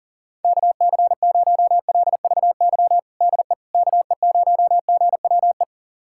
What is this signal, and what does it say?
Morse code, heard as a keyed tone, about 37 wpm, 700 Hz. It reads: KC0LVY DE KE0GWE
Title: USA Callsign DE Callsign 40wpm